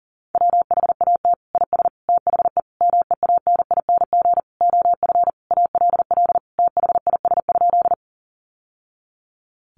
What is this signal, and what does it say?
Morse code, heard as a keyed tone, about 40 wpm, 700 Hz. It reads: WHAT IS THE MEANING OF ALL THIS?
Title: Sentences from Top 900 Words 40wpm